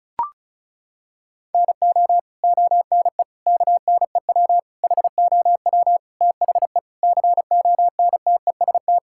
Morse code, heard as a keyed tone, about 36 wpm, 700 Hz. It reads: NO ONE KNEW HOW THE CONTEST
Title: Sentences from Top 1600 Words 35wpm